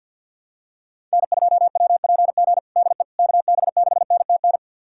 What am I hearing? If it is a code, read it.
Morse code, heard as a keyed tone, about 50 wpm, 700 Hz. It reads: N2WPG DE KB6NTN
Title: USA Callsign DE Callsign 50wpm